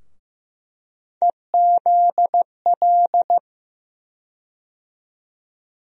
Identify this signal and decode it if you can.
Morse code, heard as a keyed tone, about 15 wpm, 700 Hz. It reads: EZL
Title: USA Call Sign Postfixes - 3 Letters 15wpm